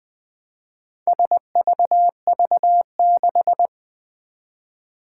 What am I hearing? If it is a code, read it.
Morse code, heard as a keyed tone, about 20 wpm, 700 Hz. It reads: SVV6
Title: Four Challenging Letter-Number Combinations 20wpm